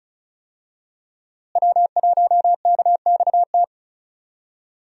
Morse code, heard as a keyed tone, about 34 wpm, 700 Hz. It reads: W1KXT
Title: USA Call Signs - 5 Characters - Repeated 2x - 35wpm